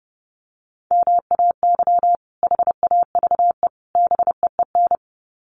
Morse code, heard as a keyed tone, about 28 wpm, 700 Hz. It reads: MAY HAVE BEEN
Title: Sets of 3 Words 30wpm